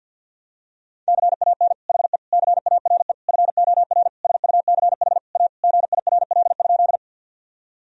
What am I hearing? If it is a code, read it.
Morse code, heard as a keyed tone, about 52 wpm, 700 Hz. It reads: CAN HE CARE FOR SUCH A GIRL?
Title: Sentences from Top 1000 Words - Review Part 1 - 50wpm